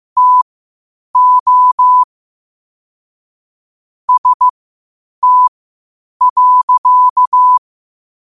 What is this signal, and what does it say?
Morse code, heard as a keyed tone, about 15 wpm, spaced as for 8 wpm, 1000 Hz. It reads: TO ST.
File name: qotdfiles / 8wpm